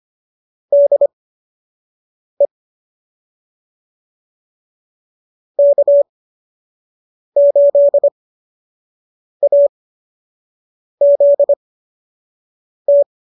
Morse code, heard as a keyed tone, about 25 wpm, spaced as for 6 wpm, 590 Hz. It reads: DE K8AZT